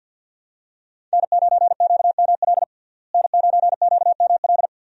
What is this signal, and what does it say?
Morse code, heard as a keyed tone, about 53 wpm, 700 Hz. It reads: N9QML N9QML
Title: USA Call Signs - 5 Characters - Repeated 2x - 50wpm